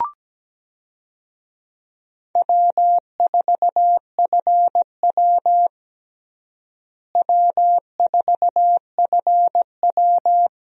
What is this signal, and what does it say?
Morse code, heard as a keyed tone, about 17 wpm, 700 Hz. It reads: W4FW W4FW
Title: USA Call Signs - 4 Characters - Repeated 2x - 17wpm